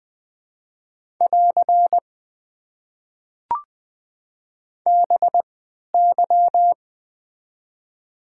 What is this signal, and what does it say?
Morse code, heard as a keyed tone, about 20 wpm, spaced as for 11 wpm, 700 Hz. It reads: <AR> BY
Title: Top 90 QSO Elements - 3x Character Spacing - 20wpm